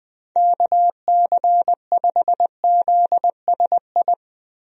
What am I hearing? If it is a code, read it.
Morse code, heard as a keyed tone, about 20 wpm, 700 Hz. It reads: KC5ZSI
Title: USA Call Signs - 6 Characters - Repeated 2x - 20wpm